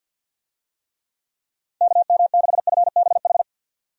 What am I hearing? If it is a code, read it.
Morse code, heard as a keyed tone, about 47 wpm, 700 Hz. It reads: KM6FBH
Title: USA Call Signs - 6 Characters - 50wpm